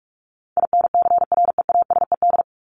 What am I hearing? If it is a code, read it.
Morse code, heard as a keyed tone, about 44 wpm, 700 Hz. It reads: INCREASED
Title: ED Suffix - Top 100 Words with Base Words 45wpm